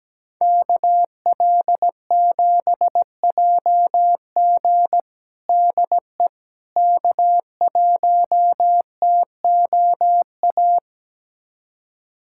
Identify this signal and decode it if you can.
Morse code, heard as a keyed tone, about 17 wpm, 700 Hz. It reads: KL7JG DE K1TOA
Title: USA Callsign DE Callsign - 5 Characters - 17wpm